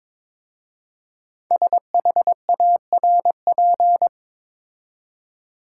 Morse code, heard as a keyed tone, about 22 wpm, 700 Hz. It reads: SHARP
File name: 5 Letter Words - First 3 Letters Given - 22wpm